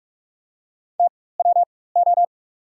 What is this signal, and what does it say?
Morse code, heard as a keyed tone, about 46 wpm, spaced as for 21 wpm, 700 Hz. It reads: TWO
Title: 4x Character Spacing - Top 300 Words - 45wpm